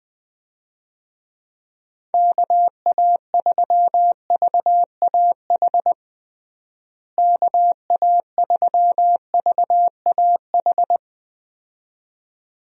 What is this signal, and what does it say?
Morse code, heard as a keyed tone, about 20 wpm, 700 Hz. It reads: KA3VAH KA3VAH
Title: USA Call Signs - 6 Characters - Repeated 2x - 20wpm